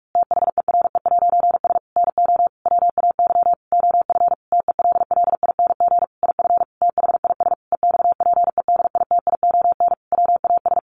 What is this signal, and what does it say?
Morse code, heard as a keyed tone, about 44 wpm, 700 Hz. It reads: THERE'S NO WAY OF TELLING IF THIS EXPEDITION WAS
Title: A Sound Of Thunder 45wpm